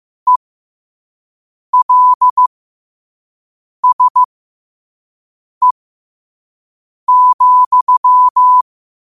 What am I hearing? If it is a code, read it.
Morse code, heard as a keyed tone, about 15 wpm, spaced as for 5 wpm, 1000 Hz. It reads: ELSE,
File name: qotdfiles / 5wpm